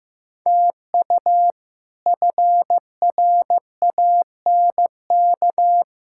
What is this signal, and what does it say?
Morse code, heard as a keyed tone, about 15 wpm, 700 Hz. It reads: TU FRANK
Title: K1USN SST QSO Practice - 15wpm